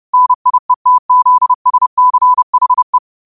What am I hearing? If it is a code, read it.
Morse code, heard as a keyed tone, about 30 wpm, 1000 Hz. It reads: NIETZSCHE
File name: qotdfiles / 30wpm